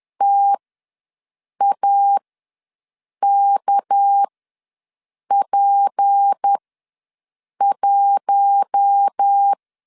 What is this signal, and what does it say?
Morse code, heard as a keyed tone, about 10 wpm, spaced as for 6 wpm, 800 Hz. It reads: TAKP1
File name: LEZIONI / 7 GF9Z / 7 GF9 ws 10wpm 4034